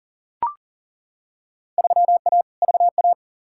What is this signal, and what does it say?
Morse code, heard as a keyed tone, about 40 wpm, 700 Hz. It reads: 3A VA
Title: Field Day Exchanges - 40wpm